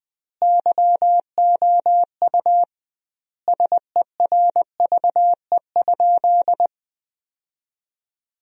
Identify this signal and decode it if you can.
Morse code, heard as a keyed tone, about 20 wpm, 700 Hz. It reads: YOU SERVE?
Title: 2x Word-Spacing - Sentences from Top 400 Words - 20wpm